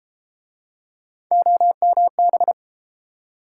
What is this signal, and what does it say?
Morse code, heard as a keyed tone, about 31 wpm, 700 Hz. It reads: OMB